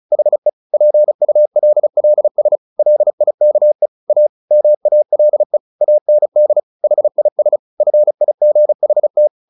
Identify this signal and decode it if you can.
Morse code, heard as a keyed tone, about 34 wpm, 600 Hz. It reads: HE PULLS LIKE A MALE AND HIS FIGHT
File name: the old man and the sea / the old man and the sea 1 35wpm